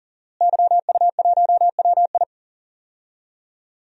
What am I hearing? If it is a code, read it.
Morse code, heard as a keyed tone, about 40 wpm, 700 Hz. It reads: YU1WI
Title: DXCC-100 Call Signs - 5 Characters - Repeated 2x - 40wpm